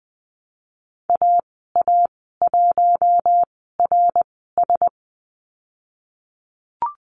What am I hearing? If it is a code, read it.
Morse code, heard as a keyed tone, about 20 wpm, spaced as for 15 wpm, 700 Hz. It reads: AA1RS
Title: 076 Lesson 13g - Letter L - Call Signs 20wpm